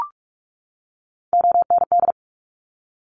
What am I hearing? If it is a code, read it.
Morse code, heard as a keyed tone, about 46 wpm, 700 Hz. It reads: OND
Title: Top Three Letter Combinations in Words 45wpm